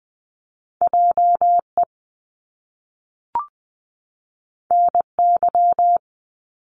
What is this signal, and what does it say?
Morse code, heard as a keyed tone, about 20 wpm, 700 Hz. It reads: JE NY
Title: Top Two Letter Combinations in Words 20wpm